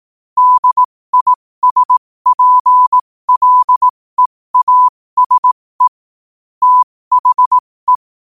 Morse code, heard as a keyed tone, about 18 wpm, spaced as for 15 wpm, 1000 Hz. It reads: DISPLEASE THE
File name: qotdfiles / 15wpm